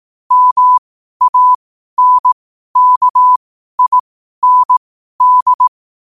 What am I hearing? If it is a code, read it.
Morse code, heard as a keyed tone, about 18 wpm, spaced as for 12 wpm, 1000 Hz. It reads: MANKIND
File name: qotdfiles / 12wpm